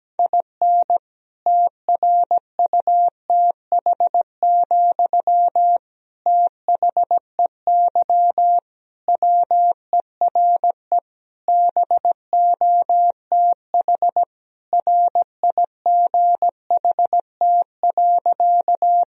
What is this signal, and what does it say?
Morse code, heard as a keyed tone, about 17 wpm, 700 Hz. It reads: IN TRUTH, THEY WERE BOTH RIGHT.
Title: Sentences from Top 1000 Words - Review Part 1 - 17wpm